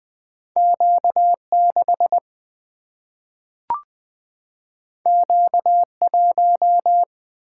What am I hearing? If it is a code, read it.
Morse code, heard as a keyed tone, about 20 wpm, 700 Hz. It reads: Q 6 Q 1
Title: Single-Letter Single-Number 20wpm